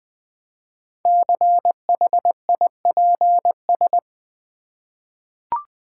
Morse code, heard as a keyed tone, about 20 wpm, 700 Hz. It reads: CHIPS
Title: Easy Words with Hints - 20wpm